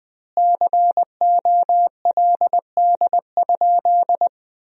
Morse code, heard as a keyed tone, about 21 wpm, 700 Hz. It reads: COLD?
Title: 4x Word-Spacing - Sentences from Top 400 Words - 20wpm